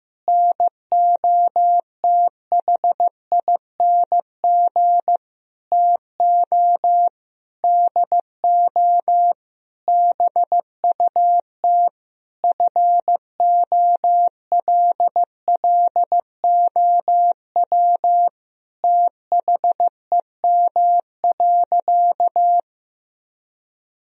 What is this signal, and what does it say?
Morse code, heard as a keyed tone, about 15 wpm, 700 Hz. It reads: NOTHING TO DO BUT FOLLOW THEM.
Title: Sentences from Top 1000 Words - Review Part 1 - 15wpm